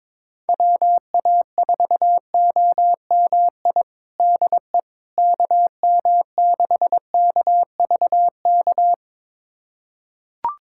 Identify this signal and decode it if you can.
Morse code, heard as a keyed tone, about 22 wpm, 700 Hz. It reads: WA4OMI DE KM6KVK
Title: USA Callsign DE Callsign - 6 Characters - 22wpm